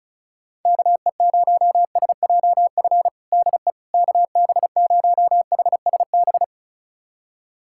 Morse code, heard as a keyed tone, about 36 wpm, 700 Hz. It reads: KE0SJF DE KB0HSB
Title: USA Callsign DE Callsign - 6 Characters - 35wpm